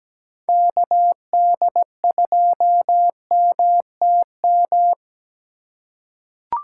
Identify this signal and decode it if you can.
Morse code, heard as a keyed tone, about 17 wpm, 700 Hz. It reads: KD2MTM
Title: USA Call Signs - 6 Characters - 3x Character Spacing - 17wpm